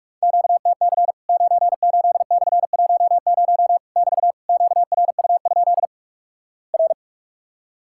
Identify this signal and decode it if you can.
Morse code, heard as a keyed tone, about 46 wpm, 700 Hz. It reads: QTC 98/10 <BT> QRU? R
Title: Worked All Europe DX Contest - QTC Exchange - 45wpm